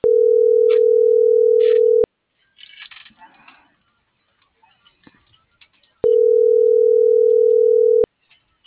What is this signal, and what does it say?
Ringback tone (North America), 440 + 480 Hz.